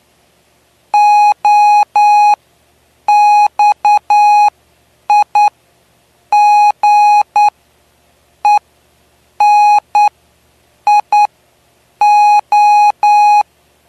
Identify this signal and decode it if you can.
Morse code, heard as a keyed tone, about 10 wpm, spaced as for 7 wpm, 830 Hz. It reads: OXIGENIO
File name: cw / py2lcd / 09-PY2LCD-Li��o 9